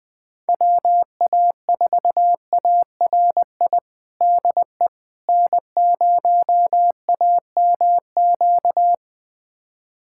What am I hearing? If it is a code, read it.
Morse code, heard as a keyed tone, about 20 wpm, 700 Hz. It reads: WA4ARI DE N0AMQ
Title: USA Callsign DE Callsign 20wpm